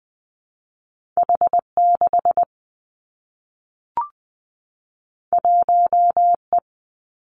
Morse code, heard as a keyed tone, about 20 wpm, 700 Hz. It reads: H 6 1 E
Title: Two Challenging Letter-Number Combinations 20wpm